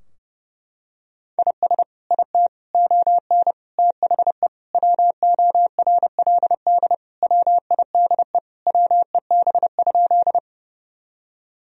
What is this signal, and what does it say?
Morse code, heard as a keyed tone, about 30 wpm, 700 Hz. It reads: IS IT ON THE WORLD WIDE WEB?
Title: Sentences from Top 1000 Words - Review Part 4 - 30wpm